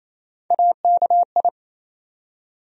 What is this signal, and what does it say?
Morse code, heard as a keyed tone, about 28 wpm, 700 Hz. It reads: AKI
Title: USA Call Sign Postfixes - 3 Letters 28wpm